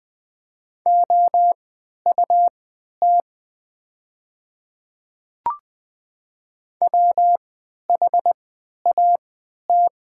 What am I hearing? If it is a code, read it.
Morse code, heard as a keyed tone, about 20 wpm, spaced as for 11 wpm, 700 Hz. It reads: OUT WHAT
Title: 3x Character Spacing - Top 500 Words - 20wpm